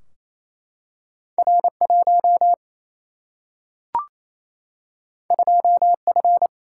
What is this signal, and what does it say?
Morse code, heard as a keyed tone, about 28 wpm, 700 Hz. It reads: R1 2F
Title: Single-Letter Single-Number 28wpm